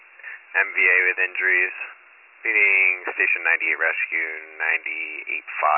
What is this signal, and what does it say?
Emergency Alert System (SAME) tones, heard as harsh buzzing bursts, with speech. It is heard from 0.5 to 4.6 s.